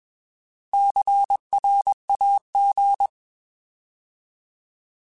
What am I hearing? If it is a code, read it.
Morse code, heard as a keyed tone, about 21 wpm, 800 Hz. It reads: CRAG